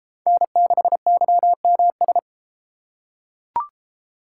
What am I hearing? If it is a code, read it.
Morse code, heard as a keyed tone, about 30 wpm, 700 Hz. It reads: N6YMS